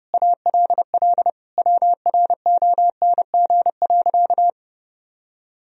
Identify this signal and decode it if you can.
Morse code, heard as a keyed tone, about 32 wpm, 700 Hz. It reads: ALL WRONG.
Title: Sentences from Top 1000 Words - Review Part 3 - 30wpm